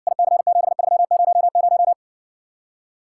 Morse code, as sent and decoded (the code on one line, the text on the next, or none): .. -.-. --... ...-- ----- -----
IC7300